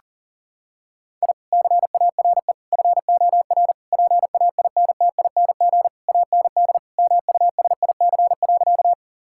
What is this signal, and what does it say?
Morse code, heard as a keyed tone, about 40 wpm, 700 Hz. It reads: I CARE FOR PAINTING AND MUSIC.
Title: Sentences from Top 1000 Words - Review Part 6 - 40wpm